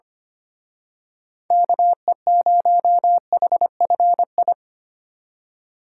Morse code, heard as a keyed tone, about 30 wpm, 700 Hz. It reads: KE0HFI
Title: Speed Racing - USA Call Signs 25wpm